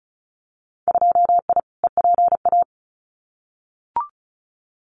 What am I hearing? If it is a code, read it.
Morse code, heard as a keyed tone, about 34 wpm, 700 Hz. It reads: 2I EPA
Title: Winter Field Day Exchanges 35wpm